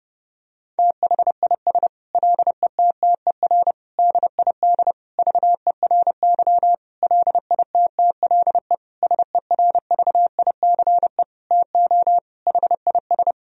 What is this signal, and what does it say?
Morse code, heard as a keyed tone, about 32 wpm, 700 Hz. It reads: THIS LETTER DID VERY LITTLE SERVICE TO HIS
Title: Sentences from Top 1000 Words - Review Part 4 - 30wpm